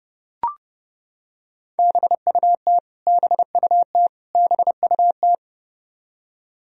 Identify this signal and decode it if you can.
Morse code, heard as a keyed tone, about 30 wpm, 700 Hz. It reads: BUT BUT BUT
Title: Top 100 Words Repeated 30wpm